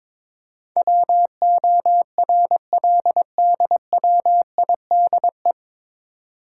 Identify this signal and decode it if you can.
Morse code, heard as a keyed tone, about 22 wpm, 700 Hz. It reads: WORLDWIDE
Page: Morse Code Ninja practice 22wpm